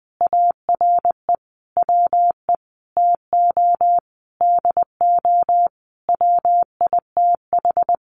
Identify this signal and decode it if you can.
Morse code, heard as a keyed tone, about 20 wpm, 700 Hz. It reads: ARE WE TO DO WITH
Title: Sentences from Top 100 Words 20wpm